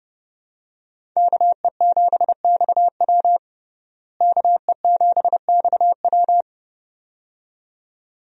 Morse code, heard as a keyed tone, about 32 wpm, 700 Hz. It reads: KE7XW KE7XW
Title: USA Call Signs - 5 Characters - Repeated 2x - 30wpm